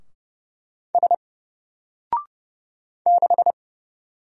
Morse code, heard as a keyed tone, about 30 wpm, 700 Hz. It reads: S 6
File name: Single-Letter-Number-30wpm